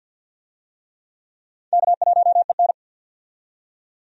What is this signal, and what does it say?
Morse code, heard as a keyed tone, about 53 wpm, 700 Hz. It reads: K1EN
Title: USA Call Signs - 4 Characters - 50wpm